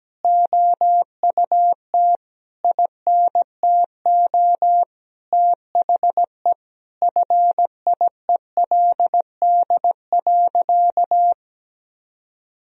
Morse code, heard as a keyed tone, about 17 wpm, 700 Hz. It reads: OUT INTO THE FIELD.
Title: Sentences from Top 1000 Words - Review Part 2 - 17wpm